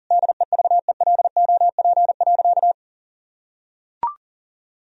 Morse code, heard as a keyed tone, about 40 wpm, 700 Hz. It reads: DEVELOP.
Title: Sentences from Top 1000 Words - Review Part 3 - 40wpm